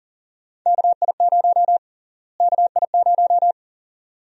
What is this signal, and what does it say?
Morse code, heard as a keyed tone, about 40 wpm, 700 Hz. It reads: KI0 KI0
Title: USA Call Sign Prefixes - Repeated 2x - 40wpm